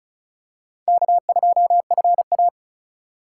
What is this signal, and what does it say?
Morse code, heard as a keyed tone, about 36 wpm, 690 Hz. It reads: K2FA
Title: USA Call Signs - 4 Characters - 2x Character Spacing - 35wpm